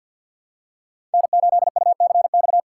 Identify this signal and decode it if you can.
Morse code, heard as a keyed tone, about 51 wpm, 700 Hz. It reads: N8UCX
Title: USA Call Signs - 5 Characters - 2x Character Spacing - 50wpm